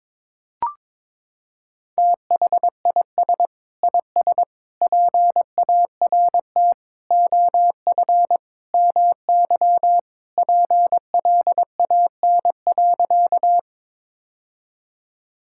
Morse code, heard as a keyed tone, about 22 wpm, 700 Hz. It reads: THIS IS PART OF MY PLAN.
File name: practice / 22wpm / Sentences with 2-Word Phrases - 22wpm